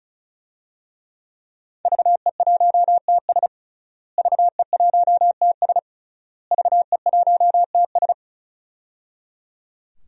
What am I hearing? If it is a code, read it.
Morse code, heard as a keyed tone, about 36 wpm, 700 Hz. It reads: VE1TS VE1TS VE1TS
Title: DXCC-100 Call Signs - 5 Characters - Repeated 3x - 35wpm